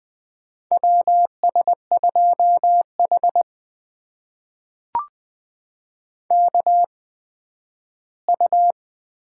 Morse code, heard as a keyed tone, about 20 wpm, spaced as for 12 wpm, 700 Hz. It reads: WS2H KU
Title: USA Call Signs - 4 Characters - 8x Character Spacing - 20wpm